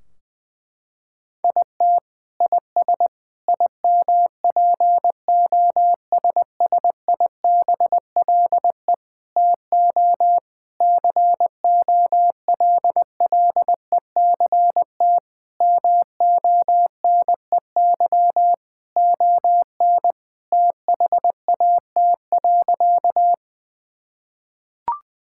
Morse code, heard as a keyed tone, about 20 wpm, 700 Hz. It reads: IT IS IMPOSSIBLE TO COLLECT MONEY ON THAT.
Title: Sentences from Top 1600 Words 20wpm